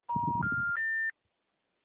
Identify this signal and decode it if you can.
Special information tone.